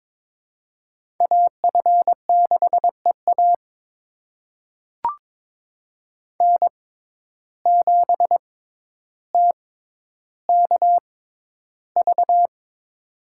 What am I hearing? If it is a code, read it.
Morse code, heard as a keyed tone, about 22 wpm, spaced as for 11 wpm, 700 Hz. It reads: AF6EA N7TKV
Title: USA Call Signs - 5 Characters - 6x Character Spacing - 22wpm